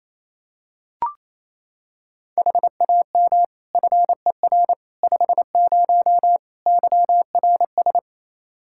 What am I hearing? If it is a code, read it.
Morse code, heard as a keyed tone, about 28 wpm, 700 Hz. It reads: HAM FER 50 YRS
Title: Practice QSO - Exchange 3 - More Info - 28wpm